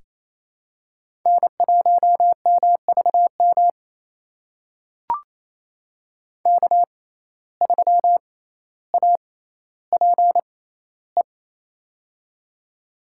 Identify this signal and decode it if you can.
Morse code, heard as a keyed tone, about 28 wpm, spaced as for 14 wpm, 700 Hz. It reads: N1MVM K3APE
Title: USA Call Signs - 5 Characters - 6x Character Spacing - 28wpm